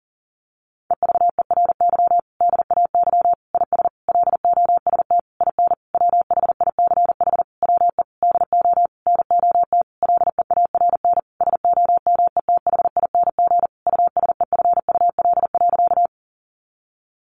Morse code, heard as a keyed tone, about 37 wpm, 700 Hz. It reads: EVERY DAY IS LOST IN WHICH WE DO NOT LEARN SOMETHING USEFUL.
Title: Sentences from Top 1400 Words 40wpm